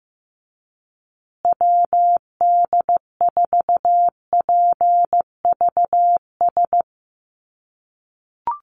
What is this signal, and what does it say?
Morse code, heard as a keyed tone, about 15 wpm, 700 Hz. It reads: WD4PVS
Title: USA Call Signs Repeated 2x 15wpm